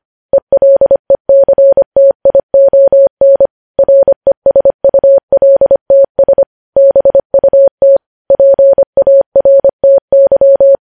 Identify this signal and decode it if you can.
Morse code, heard as a keyed tone, about 25 wpm, 550 Hz. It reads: ELECTION RESULTS BUT PARTY
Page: news headlines in morse code at 25 wpm